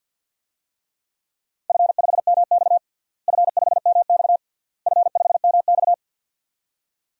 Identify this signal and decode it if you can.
Morse code, heard as a keyed tone, about 52 wpm, 700 Hz. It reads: F5MX F5MX F5MX
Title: DXCC-100 Call Signs - 4 Characters - Repeated 3x - 50wpm